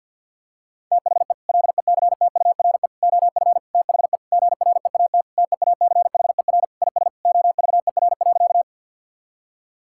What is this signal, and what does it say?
Morse code, heard as a keyed tone, about 50 wpm, 700 Hz. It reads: THE LECTURE OF THE GREAT TEACHER IS OVER.
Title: Sentences from Top 1600 Words 50wpm